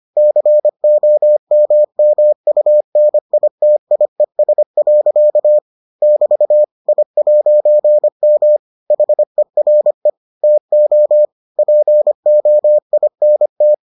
Morse code, heard as a keyed tone, about 25 wpm, 600 Hz. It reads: COMMUNITIES. <BT> I'M HERE TO POINT